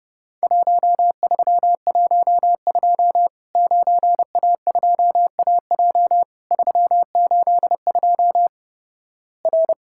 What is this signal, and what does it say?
Morse code, heard as a keyed tone, about 32 wpm, 700 Hz. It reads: 1312 9A2AJ 382 R
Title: Worked All Europe DX Contest - QTC Exchange - 30wpm